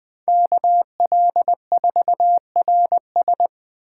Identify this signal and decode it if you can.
Morse code, heard as a keyed tone, about 21 wpm, 700 Hz. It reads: KL4RS